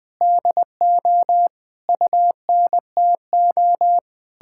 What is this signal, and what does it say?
Morse code, heard as a keyed tone, about 20 wpm, 700 Hz. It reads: DO UNTO
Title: Sentences with 8-Word Phrases - 20wpm